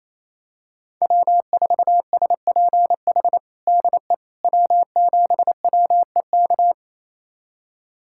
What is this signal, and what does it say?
Morse code, heard as a keyed tone, about 28 wpm, 700 Hz. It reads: W4SPH DE W7WEK
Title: USA Callsign DE Callsign - 5 Characters - 28wpm